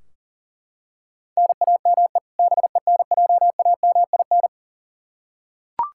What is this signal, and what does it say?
Morse code, heard as a keyed tone, about 40 wpm, 700 Hz. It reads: NAME BENJAMIN
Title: K1USN SST QSO Practice 40wpm